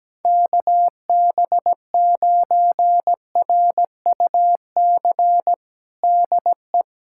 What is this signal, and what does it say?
Morse code, heard as a keyed tone, about 17 wpm, 700 Hz. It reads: KB9RUC DE
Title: USA Callsign DE Callsign - 6 Characters - 17wpm